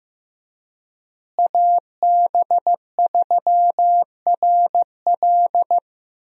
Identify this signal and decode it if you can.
Morse code, heard as a keyed tone, about 15 wpm, 700 Hz. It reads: AB3RL
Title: USA Call Signs - 5 Characters - Repeated 3x - 15wpm